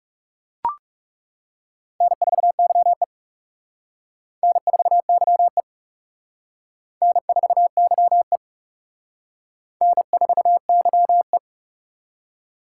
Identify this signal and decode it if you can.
Morse code, heard as a keyed tone, about 36 wpm, 700 Hz. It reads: N4YE N4YE N4YE N4YE
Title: Speed Racing - USA 1x2 Call Signs 30wpm